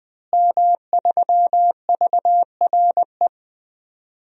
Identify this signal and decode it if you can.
Morse code, heard as a keyed tone, about 21 wpm, 700 Hz. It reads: M3VRE
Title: DXCC-100 Call Signs - 5 Characters - Repeated 3x - 20wpm